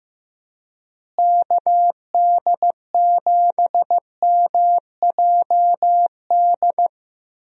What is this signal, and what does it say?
Morse code, heard as a keyed tone, about 15 wpm, 700 Hz. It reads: KD7MJD
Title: USA Call Signs - 6 Characters - Repeated 3x - 15wpm